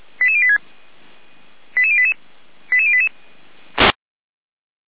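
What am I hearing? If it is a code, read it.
Five-tone selective call (ZVEI), heard as quick runs of five tones, tones of about 70 ms. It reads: 88086 88088 88088
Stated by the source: ZVEI five-tone call 88098 received on an SDR.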